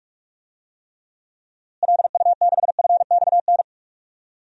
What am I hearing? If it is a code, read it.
Morse code, heard as a keyed tone, about 44 wpm, 700 Hz. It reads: LU6FXN
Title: DXCC-100 Call Signs - 6 Characters - Repeated 2x - 45wpm